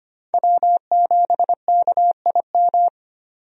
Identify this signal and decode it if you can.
Morse code, heard as a keyed tone, about 25 wpm, 700 Hz. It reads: W7KIM
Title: USA Call Signs - 5 Characters - 25wpm